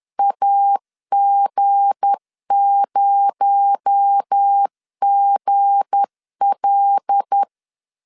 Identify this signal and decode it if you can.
Morse code, heard as a keyed tone, about 10 wpm, 800 Hz. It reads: AG0GL
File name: LEZIONI / 7 GF9Z / 7 G 10wpm 2501